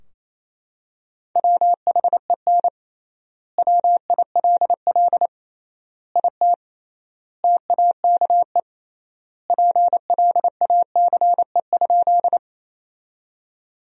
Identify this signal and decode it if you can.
Morse code, heard as a keyed tone, about 28 wpm, 700 Hz. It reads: WHEN WILL IT TAKE PLACE?
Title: 3x Word-Spacing - Sentences from Top 200 Words - 28wpm